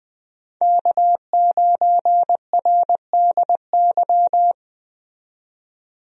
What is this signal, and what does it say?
Morse code, heard as a keyed tone, about 21 wpm, 700 Hz. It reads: K9RDY
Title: USA Call Signs - 5 Characters - 20wpm